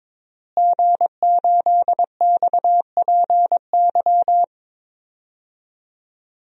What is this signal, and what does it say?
Morse code, heard as a keyed tone, about 22 wpm, 700 Hz. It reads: G8XPY
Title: DXCC-100 Call Signs - 5 Characters - Repeated 3x - 22wpm